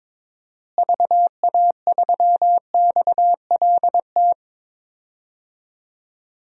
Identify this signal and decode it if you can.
Morse code, heard as a keyed tone, about 22 wpm, 700 Hz. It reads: VA3XLT
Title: DXCC-100 Call Signs - 6 Characters - Repeated 3x - 22wpm